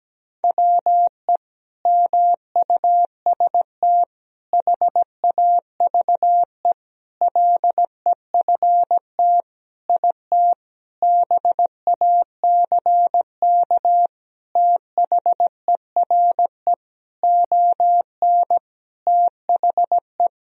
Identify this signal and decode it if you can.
Morse code, heard as a keyed tone, about 17 wpm, 700 Hz. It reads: WE MUST HAVE LEFT IT BACK THERE ON THE
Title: Sentences from Top 1000 Words - Review Part 2 - 17wpm